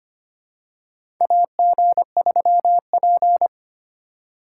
Morse code, heard as a keyed tone, about 25 wpm, 700 Hz. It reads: AG3P
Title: USA Call Signs - 4 Characters - 4x Character Spacing - 25wpm